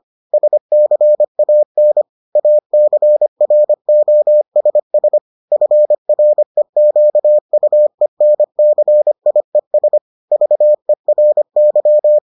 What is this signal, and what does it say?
Morse code, heard as a keyed tone, about 25 wpm, 600 Hz. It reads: SCAN ACROSS FREQUENCIES VERY